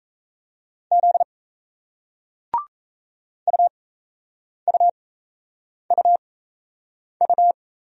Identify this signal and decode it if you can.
Morse code, heard as a keyed tone, about 35 wpm, spaced as for 8 wpm, 700 Hz. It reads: Z UUUU